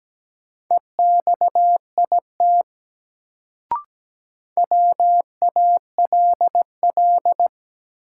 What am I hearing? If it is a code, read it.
Morse code, heard as a keyed tone, about 17 wpm, 700 Hz. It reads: EXIT WALL
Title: Sentences from Top 1100 Words - New Word Drill Alt - 17wpm